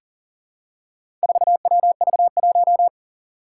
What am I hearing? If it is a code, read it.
Morse code, heard as a keyed tone, about 40 wpm, 700 Hz. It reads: 4WV1
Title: Four Challenging Letter-Number Combinations 40wpm